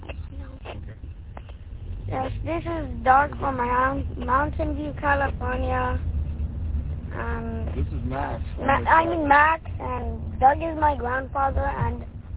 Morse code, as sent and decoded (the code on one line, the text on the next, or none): none